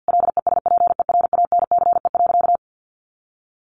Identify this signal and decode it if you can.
Morse code, heard as a keyed tone, about 50 wpm, 700 Hz. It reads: LESPERANCE.
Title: A Sound Of Thunder 50wpm